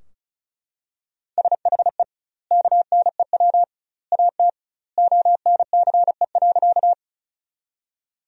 Morse code, heard as a keyed tone, about 36 wpm, 700 Hz. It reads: SHE KNEW AT ONCE.
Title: 2x Word-Spacing - Sentences from Top 400 Words - 35wpm